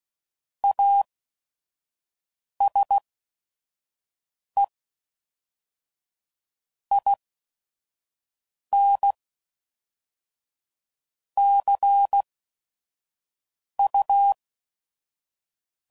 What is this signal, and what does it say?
Morse code, heard as a keyed tone, about 16 wpm, 800 Hz. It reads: A S E I N C U